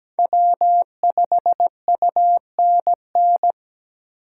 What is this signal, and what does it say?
Morse code, heard as a keyed tone, about 17 wpm, 700 Hz. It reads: W5UNN
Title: USA Call Signs - 5 Characters - 8x Character Spacing - 17wpm